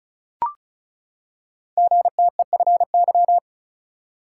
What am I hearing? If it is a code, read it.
Morse code, heard as a keyed tone, about 36 wpm, 690 Hz. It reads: GTEFY
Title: Code Groups - 5 Letter:Numbers - 35wpm